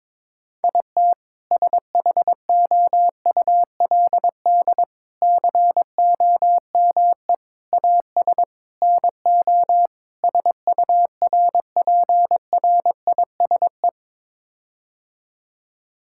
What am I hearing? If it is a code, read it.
Morse code, heard as a keyed tone, about 22 wpm, 700 Hz. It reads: IT SHOULD COME AS NO SURPRISE
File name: practice / 22wpm / Sets of 6 Words - 22wpm